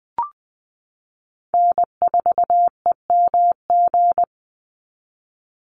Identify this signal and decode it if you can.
Morse code, heard as a keyed tone, about 20 wpm, 700 Hz. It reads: N4EMG
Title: NA SOTA Chaser Callsigns 20wpm